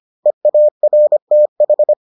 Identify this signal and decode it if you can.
Morse code, heard as a keyed tone, about 25 wpm, 600 Hz. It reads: EARTH